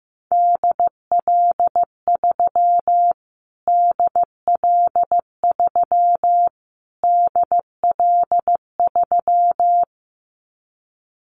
Morse code, heard as a keyed tone, about 15 wpm, 700 Hz. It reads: DL3 DL3 DL3
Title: Callsign Prefixes for Top 25 DXCC - Repeated 3x - 15wpm